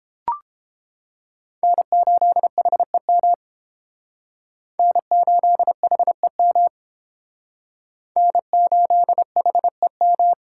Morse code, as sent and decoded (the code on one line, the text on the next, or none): -. ---.. .... -- / -. ---.. .... -- / -. ---.. .... --
N8HEM N8HEM N8HEM